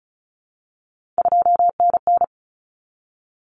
Morse code, heard as a keyed tone, about 34 wpm, 700 Hz. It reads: 2NN
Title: Common Cut Numbers - 3-Digit 35wpm